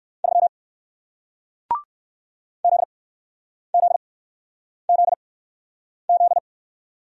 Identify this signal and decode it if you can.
Morse code, heard as a keyed tone, about 57 wpm, spaced as for 9 wpm, 700 Hz. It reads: 3 ZZZZ